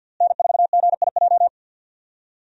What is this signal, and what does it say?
Morse code, heard as a keyed tone, about 53 wpm, 700 Hz. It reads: N4GIJ
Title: USA Call Signs - 5 Characters - Repeated 2x - 50wpm